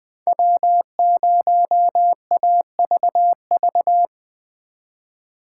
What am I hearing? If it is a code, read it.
Morse code, heard as a keyed tone, about 20 wpm, 700 Hz. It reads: W0AVV
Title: Speed Racing - USA 1x3 Call Signs 17wpm